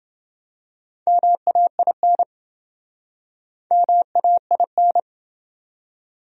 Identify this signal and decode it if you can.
Morse code, heard as a keyed tone, about 26 wpm, 700 Hz. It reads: MAIN MAIN